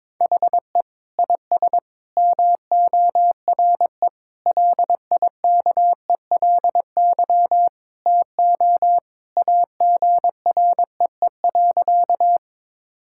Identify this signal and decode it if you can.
Morse code, heard as a keyed tone, about 22 wpm, 700 Hz. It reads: HE IS MORE LIKELY TO AGREE.
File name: practice / 22wpm / Sentences with 3-Word Phrases - 22wpm